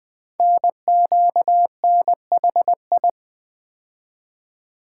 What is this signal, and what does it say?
Morse code, heard as a keyed tone, about 20 wpm, 700 Hz. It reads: NQNHI
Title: Code Groups - 5 Letters - 20wpm